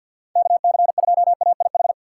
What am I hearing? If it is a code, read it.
Morse code, heard as a keyed tone, about 52 wpm, 700 Hz. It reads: KC2AIH
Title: USA Call Signs - 6 Characters - 6x Character Spacing - 50wpm